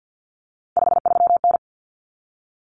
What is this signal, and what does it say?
Morse code, heard as a keyed tone, about 50 wpm, 700 Hz. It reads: 53N
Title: Common Cut Numbers - 3-Digit 50wpm